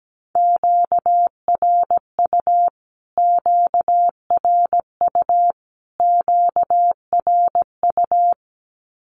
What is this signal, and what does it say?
Morse code, heard as a keyed tone, about 17 wpm, 700 Hz. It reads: QRU QRU QRU
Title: Q-Codes Repeated 3x 17wpm